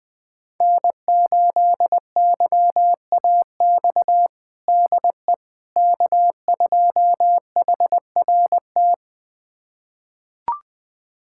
Morse code, heard as a keyed tone, about 20 wpm, 700 Hz. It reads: N8YAX DE K2HRT
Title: USA Callsign DE Callsign - 5 Characters - 20wpm